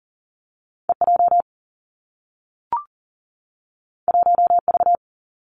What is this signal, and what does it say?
Morse code, heard as a keyed tone, about 42 wpm, 700 Hz. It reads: E J 1 V
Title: Two Challenging Letter-Number Combinations 40wpm